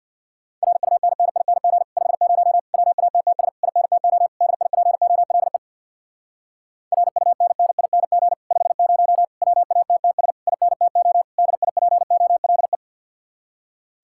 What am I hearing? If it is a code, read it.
Morse code, heard as a keyed tone, about 64 wpm, 700 Hz. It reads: RUNNING 50 WATTS INTO DIPOLE RUNNING 50 WATTS INTO DIPOLE